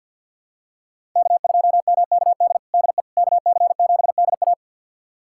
Morse code, heard as a keyed tone, about 47 wpm, 700 Hz. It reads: K2MKD DE KC7DA
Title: USA Callsign DE Callsign - 5 Characters - 50wpm